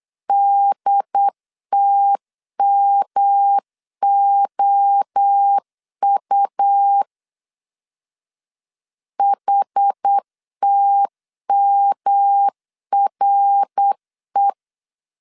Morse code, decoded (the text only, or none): DTMOU HTMRE